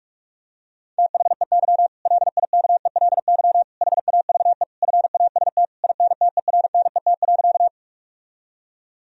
Morse code, heard as a keyed tone, about 46 wpm, 700 Hz. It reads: THEY LIKELY HAVE FAST INTERNET.
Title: Sentences from Top 1000 Words - Review Part 4 - 45wpm